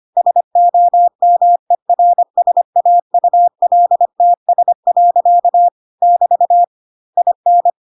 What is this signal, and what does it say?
Morse code, heard as a keyed tone, about 25 wpm, 700 Hz. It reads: SOMERSAULTS. <BT> IN